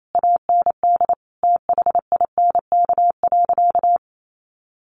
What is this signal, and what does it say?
Morse code, heard as a keyed tone, about 28 wpm, 700 Hz. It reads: AND THINK.
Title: Sentences from Top 500 Words 28wpm